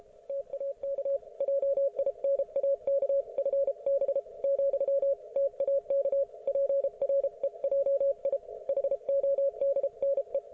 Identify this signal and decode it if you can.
Morse code, heard as a keyed tone, about 33 wpm, 550 Hz. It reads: TAK JINAK FB , TAK PREJI HODNE
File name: Archiv / CUC157 / ok1fhi cul157